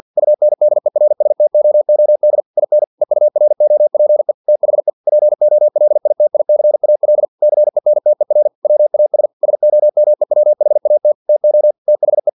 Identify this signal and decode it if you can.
Morse code, heard as a keyed tone, about 50 wpm, 600 Hz. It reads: UNDERSTOOD IN EUROPE THE POLITICAL CENTER WAS SOMEWHAT TO THE
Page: Never 49wpm